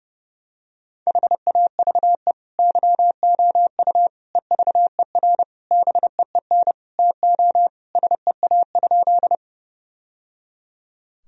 Morse code, heard as a keyed tone, about 29 wpm, 700 Hz. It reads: HAVE YOU EVER BEEN TO SEA?
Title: Sentences from Top 1000 Words - Review Part 1 - 30wpm